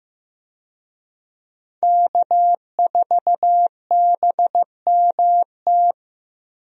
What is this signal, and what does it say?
Morse code, heard as a keyed tone, about 16 wpm, 700 Hz. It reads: K4BMT
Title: USA Call Signs - 5 Characters - Repeated 3x - 15wpm